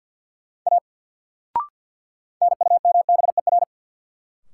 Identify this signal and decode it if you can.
Morse code, heard as a keyed tone, about 52 wpm, 700 Hz. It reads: A NUMBER
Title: Sentences from Top 100 Words - New Word Drill - 50wpm